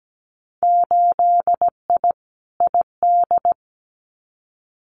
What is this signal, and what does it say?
Morse code, heard as a keyed tone, about 17 wpm, 700 Hz. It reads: 8I ID
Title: Winter Field Day Exchanges Encore 17wpm